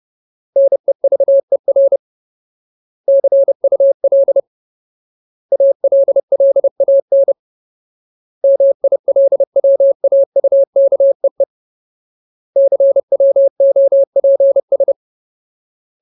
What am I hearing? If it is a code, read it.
Morse code, heard as a keyed tone, about 30 wpm, 550 Hz. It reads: NEVER CUL ALLAN MILWAUKEE CWOPS
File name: Ultimate Word List / 30WPM Ultimate